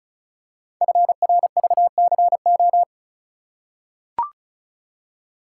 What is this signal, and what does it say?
Morse code, heard as a keyed tone, about 36 wpm, 700 Hz. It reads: FRVCO
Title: Code Groups - 5 Letters - 35wpm